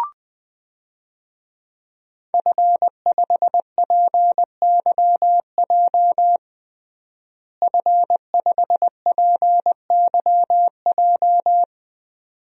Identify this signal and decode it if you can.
Morse code, heard as a keyed tone, about 20 wpm, 700 Hz. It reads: F5PYJ F5PYJ
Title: DXCC-100 Call Signs - 5 Characters - Repeated 2x - 20wpm